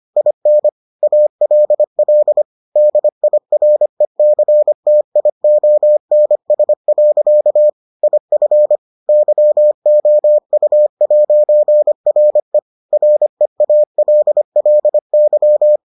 Morse code, heard as a keyed tone, about 25 wpm, 600 Hz. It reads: IN ALL DIRECTIONS. IF YOU'RE REALLY